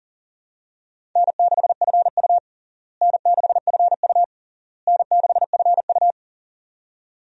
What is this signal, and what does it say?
Morse code, heard as a keyed tone, about 40 wpm, 700 Hz. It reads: N6FU N6FU N6FU
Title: USA Call Signs - 4 Characters - Repeated 3x - 40wpm